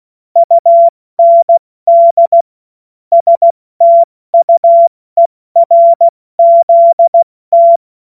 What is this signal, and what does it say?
Morse code, heard as a keyed tone, about 16 wpm, 680 Hz. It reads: UND STUERZT
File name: Morsekurs / Witze / Witz 14. Dezember 2021, 70 F80